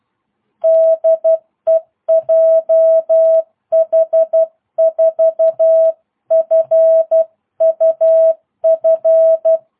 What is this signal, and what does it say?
Morse code, heard as a keyed tone, about 12 wpm, 650 Hz. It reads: DE JH4FUF